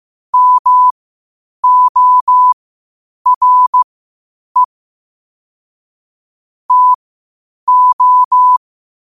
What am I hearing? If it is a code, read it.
Morse code, heard as a keyed tone, about 15 wpm, spaced as for 8 wpm, 1000 Hz. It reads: MORE TO
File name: qotdfiles / 8wpm